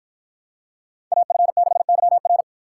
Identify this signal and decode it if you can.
Morse code, heard as a keyed tone, about 53 wpm, 700 Hz. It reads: AF6YR